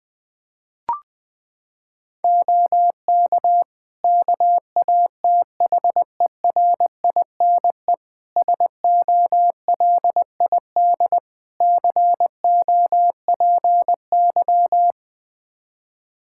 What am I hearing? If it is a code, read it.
Morse code, heard as a keyed tone, about 20 wpm, 700 Hz. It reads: OK KATHERINE SOLID COPY
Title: Practice QSO - Exchange 3 - More Info - 20wpm